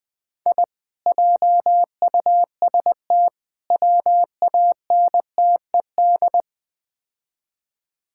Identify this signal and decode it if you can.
Morse code, heard as a keyed tone, about 20 wpm, 700 Hz. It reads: I JUST WANTED
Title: Build A Phrase - 4-Word Phrases - 20wpm